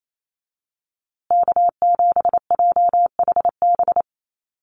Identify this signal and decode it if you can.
Morse code, heard as a keyed tone, about 28 wpm, 700 Hz. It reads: K7JHB
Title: USA Call Signs Repeated 2x 28wpm